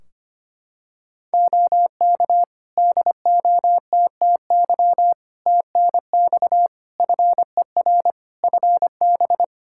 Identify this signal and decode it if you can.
Morse code, heard as a keyed tone, about 25 wpm, 700 Hz. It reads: OK DOTTY TNX FER FB
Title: Practice QSO - Exchange 4 - Ending - 25wpm